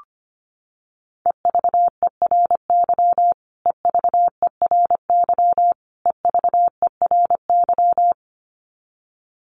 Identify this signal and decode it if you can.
Morse code, heard as a keyed tone, about 25 wpm, 700 Hz. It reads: EVERY EVERY EVERY
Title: Top 200 Words Repeated 25wpm